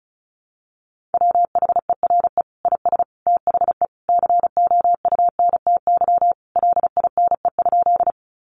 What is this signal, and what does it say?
Morse code, heard as a keyed tone, about 34 wpm, 700 Hz. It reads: WHERE IS THE COUNTY LINE?
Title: Sentences from Top 1200 Words 35wpm